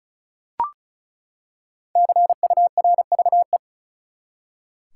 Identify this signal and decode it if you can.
Morse code, heard as a keyed tone, about 34 wpm, 700 Hz. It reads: CURVE
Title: Five Letter Words - 35wpm